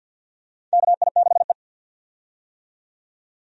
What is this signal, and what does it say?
Morse code, heard as a keyed tone, about 53 wpm, 700 Hz. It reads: KI6E
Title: USA Call Signs - 4 Characters - 50wpm